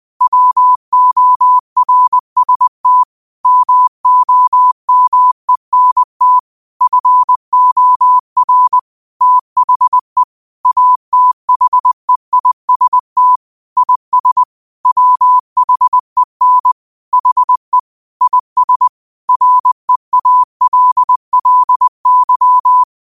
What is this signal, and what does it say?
Morse code, heard as a keyed tone, about 20 wpm, 1000 Hz. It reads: WORST MOMENT FOR THE ATHEIST IS WHEN HE IS REALLY
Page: Quotes for Fri, 15 Aug 2025 in Morse Code at 20 words per minute.